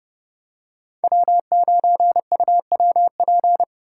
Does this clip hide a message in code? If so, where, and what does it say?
Yes: Morse code from 1.0 to 3.6 s, heard as a keyed tone, about 29 wpm, 700 Hz. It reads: W9UWP